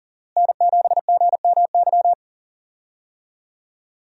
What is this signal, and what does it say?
Morse code, heard as a keyed tone, about 41 wpm, 700 Hz. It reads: N7GMY
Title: USA Call Signs - 5 Characters - Repeated 3x - 40wpm